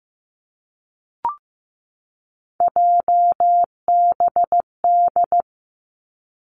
Morse code, heard as a keyed tone, about 15 wpm, 700 Hz. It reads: JBD
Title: Three Challenging Letter-Number Combinations 15wpm